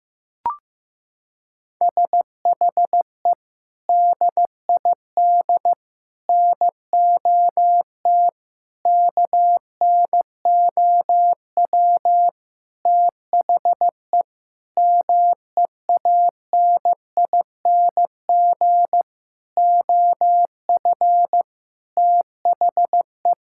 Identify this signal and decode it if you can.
Morse code, heard as a keyed tone, about 15 wpm, 700 Hz. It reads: SHE DID NOT KNOW THE MEANING OF THE
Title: Sentences from Top 1000 Words - Review Part 6 - 15wpm